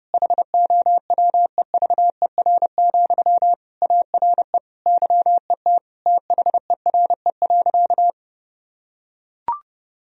Morse code, heard as a keyed tone, about 31 wpm, 700 Hz. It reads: HOWEVER, ARE YET THERE.
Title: Sentences from Top 1000 Words - Review Part 1 - 30wpm